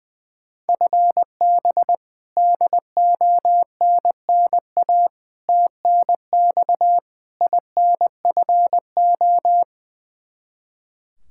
Morse code, heard as a keyed tone, about 21 wpm, 700 Hz. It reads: FB DONNA TNX INFO
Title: Practice QSO - Exchange 3 - More Info - 20wpm